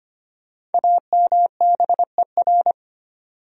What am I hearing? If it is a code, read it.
Morse code, heard as a keyed tone, about 25 wpm, 700 Hz. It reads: AMBER
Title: USA Common Names - Max 5 Letters - 25wpm